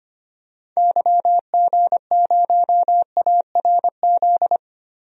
Morse code, heard as a keyed tone, about 25 wpm, 690 Hz. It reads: YG0ARZ
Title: DXCC-100 Call Signs - 6 Characters - 25wpm